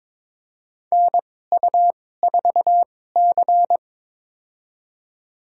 Morse code, heard as a keyed tone, about 21 wpm, spaced as for 16 wpm, 700 Hz. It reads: NU4C
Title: USA Call Signs - 4 Characters - 2x Character Spacing - 22wpm